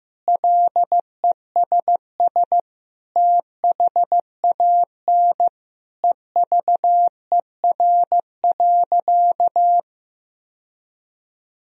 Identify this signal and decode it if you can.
Morse code, heard as a keyed tone, about 15 wpm, 700 Hz. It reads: LESS THAN EVER.
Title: Sentences from Top 1000 Words - Review Part 2 - 15wpm